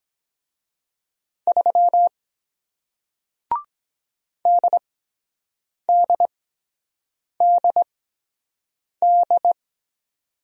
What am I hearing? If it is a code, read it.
Morse code, heard as a keyed tone, about 23 wpm, spaced as for 7 wpm, 700 Hz. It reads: 3 DDDD